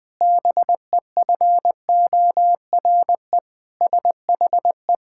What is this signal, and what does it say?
Morse code, heard as a keyed tone, about 20 wpm, 700 Hz. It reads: BEFORE SHE
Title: Pangram Challenge v2 - 20wpm